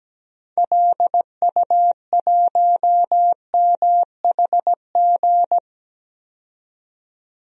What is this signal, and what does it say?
Morse code, heard as a keyed tone, about 17 wpm, 700 Hz. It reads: LU1MHG
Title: DXCC-100 Call Signs - 6 Characters - 17wpm